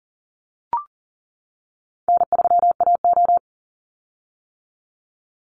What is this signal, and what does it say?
Morse code, heard as a keyed tone, about 40 wpm, 700 Hz. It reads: N3AO
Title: Easy Call Signs - 40wpm